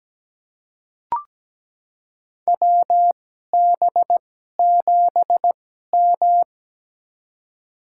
Morse code, heard as a keyed tone, about 17 wpm, spaced as for 12 wpm, 700 Hz. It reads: WB7M
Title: USA Call Signs - 4 Characters - 2x Character Spacing - 17wpm